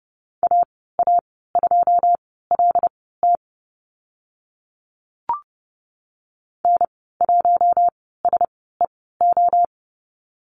Morse code, heard as a keyed tone, about 30 wpm, spaced as for 17 wpm, 700 Hz. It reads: AA2LT N1SEO
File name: Lesson 14g - Number 2 - Call Signs 30wpm